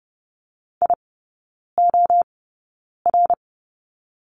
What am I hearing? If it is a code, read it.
Morse code, heard as a keyed tone, about 30 wpm, spaced as for 9 wpm, 700 Hz. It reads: IOR
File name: course / 30:10wpm / 087 Lesson 15e - Number 5 - Three Characters 30wpm